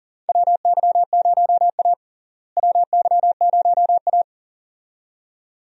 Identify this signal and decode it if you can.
Morse code, heard as a keyed tone, about 40 wpm, 700 Hz. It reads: WY0A WY0A
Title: USA Call Signs - 4 Characters - Repeated 2x - 40wpm